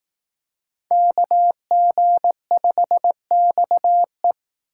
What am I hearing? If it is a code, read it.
Morse code, heard as a keyed tone, about 18 wpm, 700 Hz. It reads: KG5XE
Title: Speed Racing - USA Call Signs 15wpm